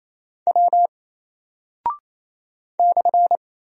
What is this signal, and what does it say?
Morse code, heard as a keyed tone, about 28 wpm, 700 Hz. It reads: W /
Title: Single Character (Letter:Number:Punctuation:Prosign) 28wpm